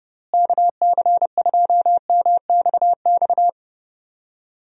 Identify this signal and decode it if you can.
Morse code, heard as a keyed tone, about 30 wpm, 700 Hz. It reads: KC2MXX
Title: USA Call Signs - 6 Characters - Repeated 3x - 30wpm